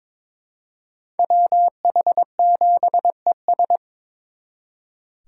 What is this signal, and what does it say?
Morse code, heard as a keyed tone, about 26 wpm, 700 Hz. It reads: WH7ES
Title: Speed Racing - USA 2x2 Call Signs 22wpm